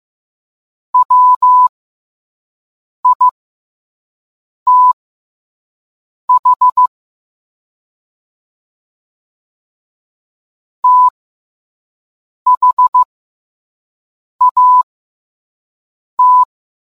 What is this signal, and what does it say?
Morse code, heard as a keyed tone, about 15 wpm, spaced as for 5 wpm, 1000 Hz. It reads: WITH THAT